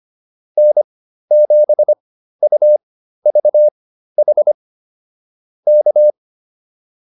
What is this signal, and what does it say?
Morse code, heard as a keyed tone, about 25 wpm, spaced as for 13 wpm, 600 Hz. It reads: N7UVH K